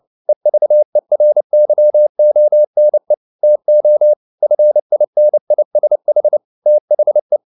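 Morse code, heard as a keyed tone, about 29 wpm, 600 Hz. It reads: EVERYONE TO FINISH THE
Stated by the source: Morse code at 29 wpm, sending Ken Follett's novel Never in English, from CW med Gnister.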